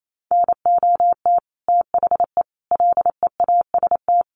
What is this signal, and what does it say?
Morse code, heard as a keyed tone, about 28 wpm, 700 Hz. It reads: NOT THE LEAST
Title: Sentences from Top 900 Words 28wpm